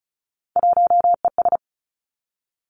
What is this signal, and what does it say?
Morse code, heard as a keyed tone, about 36 wpm, 700 Hz. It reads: 1ES
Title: Three Challenging Letter-Number Combinations 35wpm